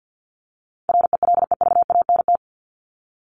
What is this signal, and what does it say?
Morse code, heard as a keyed tone, about 50 wpm, 700 Hz. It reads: RELEVANT
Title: Top 1001-1250 Words 50wpm